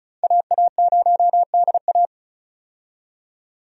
Morse code, heard as a keyed tone, about 36 wpm, 700 Hz. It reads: AA0DA
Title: USA Call Signs - 5 Characters - 35wpm